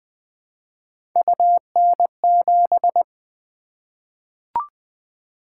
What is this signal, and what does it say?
Morse code, heard as a keyed tone, about 20 wpm, 700 Hz. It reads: UN7
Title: Three Letter-Number Combinations - 20wpm